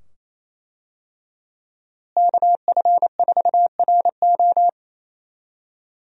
Morse code, heard as a keyed tone, about 27 wpm, 700 Hz. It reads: KF4RO
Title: USA Call Signs - 5 Characters - Repeated 2x - 28wpm